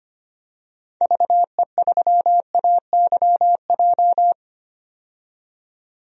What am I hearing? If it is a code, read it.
Morse code, heard as a keyed tone, about 25 wpm, 700 Hz. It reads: VE3AYJ
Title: DXCC-100 Call Signs - 6 Characters - 25wpm